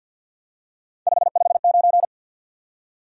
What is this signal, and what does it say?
Morse code, heard as a keyed tone, about 53 wpm, 700 Hz. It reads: 559
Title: Top 90 QSO Elements 50wpm